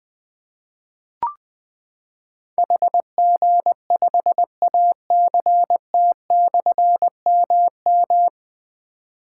Morse code, heard as a keyed tone, about 20 wpm, 700 Hz. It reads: HG5ACT/MM
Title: Difficult Call Signs 20wpm